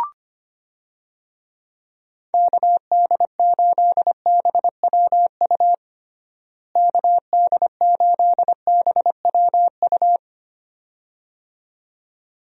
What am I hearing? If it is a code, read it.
Morse code, heard as a keyed tone, about 25 wpm, 700 Hz. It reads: KD8BWU KD8BWU
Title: USA Call Signs - 6 Characters - Repeated 2x - 25wpm